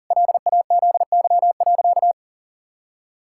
Morse code, heard as a keyed tone, about 40 wpm, 700 Hz. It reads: LAZY.
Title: Pangram Challenge v2 - 40wpm